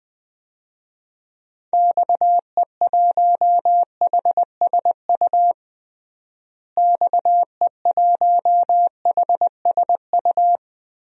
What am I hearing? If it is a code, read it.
Morse code, heard as a keyed tone, about 21 wpm, 700 Hz. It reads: XE1HSU XE1HSU
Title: DXCC-100 Call Signs - 6 Characters - Repeated 3x - 20wpm